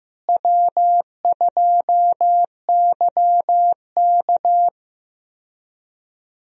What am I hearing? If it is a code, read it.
Morse code, heard as a keyed tone, about 15 wpm, 700 Hz. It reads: W2YK
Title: USA Call Signs - 4 Characters - Repeated 2x - 15wpm